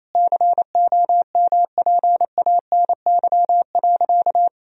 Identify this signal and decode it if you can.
Morse code, heard as a keyed tone, about 27 wpm, 700 Hz. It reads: COMPANY.
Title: Sentences from Top 1000 Words - Review Part 3 - 28wpm